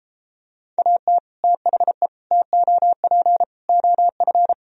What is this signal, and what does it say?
Morse code, heard as a keyed tone, about 33 wpm, 700 Hz. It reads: AT THE TOP OF